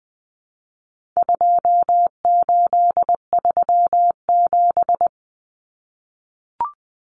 Morse code, heard as a keyed tone, about 20 wpm, 700 Hz. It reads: 2837
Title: Common Cut Numbers - 4-Digit 20wpm